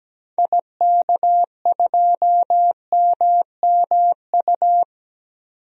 Morse code, heard as a keyed tone, about 17 wpm, 700 Hz. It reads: IK2MMU
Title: DXCC-100 Call Signs - 6 Characters - 17wpm